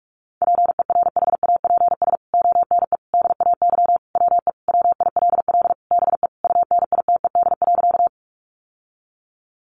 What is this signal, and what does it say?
Morse code, heard as a keyed tone, about 44 wpm, 700 Hz. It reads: PERHAPS ONE DAY WE WILL BE UNITED.
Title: Sentences from Top 600 Words 45wpm